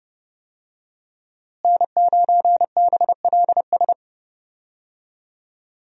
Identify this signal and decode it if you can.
Morse code, heard as a keyed tone, about 30 wpm, 700 Hz. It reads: N9BLS
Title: USA Call Signs - 5 Characters - Repeated 2x - 30wpm